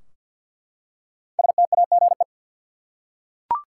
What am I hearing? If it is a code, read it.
Morse code, heard as a keyed tone, about 49 wpm, 700 Hz. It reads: STAGE
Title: Five Letter Words - 50wpm